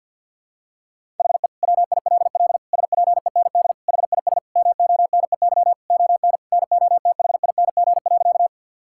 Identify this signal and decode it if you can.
Morse code, heard as a keyed tone, about 50 wpm, 700 Hz. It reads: HE WILL SPEND HIS MONEY ON NOTHING.
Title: Sentences from Top 1000 Words - Review Part 5 - 50wpm